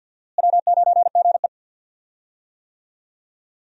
Morse code, heard as a keyed tone, about 52 wpm, 700 Hz. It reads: W9GE
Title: USA Call Signs - 4 Characters - Repeated 2x - 50wpm